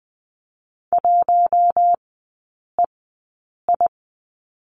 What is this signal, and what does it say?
Morse code, heard as a keyed tone, about 20 wpm, spaced as for 8 wpm, 690 Hz. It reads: 1EI
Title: 087 Lesson 15e - Number 5 - Three Characters 20wpm